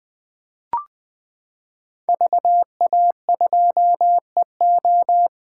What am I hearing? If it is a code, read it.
Morse code, heard as a keyed tone, about 20 wpm, 700 Hz. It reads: VA2EO
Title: 2023 SOTA Chaser Call Signs - 20wpm